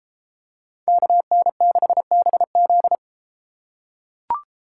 Morse code, heard as a keyed tone, about 31 wpm, 700 Hz. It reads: KN6BZ